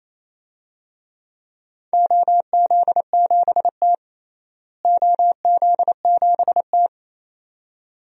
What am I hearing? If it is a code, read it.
Morse code, heard as a keyed tone, about 27 wpm, 700 Hz. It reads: OZ7T OZ7T
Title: DXCC-100 Call Signs - 4 Characters - Repeated 2x - 28wpm